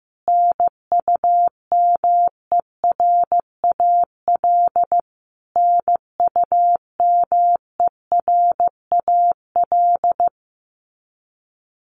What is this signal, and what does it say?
Morse code, heard as a keyed tone, about 15 wpm, 700 Hz. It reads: NUMERAL NUMERAL
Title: Top 500 Words Repeated 15wpm